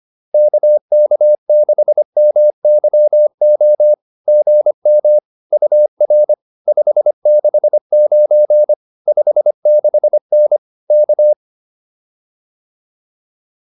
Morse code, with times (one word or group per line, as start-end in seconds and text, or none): 0.3-3.9 KK6MYO
4.3-5.2 GM
5.5-6.3 UR
6.7-8.7 569
9.1-10.6 56N
10.9-11.3 K